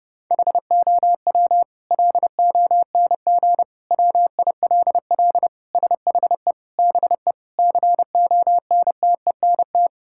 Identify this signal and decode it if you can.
Morse code, heard as a keyed tone, about 30 wpm, 700 Hz. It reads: HOW LONG WILL SHE BE CONTENT
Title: Sentences from Top 1000 Words - Review Part 6 - 30wpm